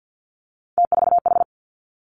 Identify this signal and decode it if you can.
Morse code, heard as a keyed tone, about 50 wpm, 700 Hz. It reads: T4H
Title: Three Challenging Letter-Number Combinations 50wpm